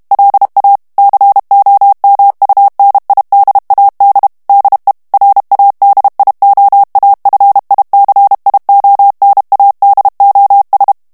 Morse code, heard as a keyed tone, about 32 wpm, 800 Hz. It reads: LA COMUNIDAD DE RADIOAFICIONADOS
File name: morse mp3 / ESPAGNOL TEXTE EN CLAIR 1 VITESSE 30 MOTS MINUTES